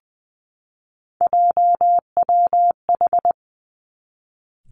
Morse code, heard as a keyed tone, about 20 wpm, 700 Hz. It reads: JWH
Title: Three Challenging Letter-Number Combinations 20wpm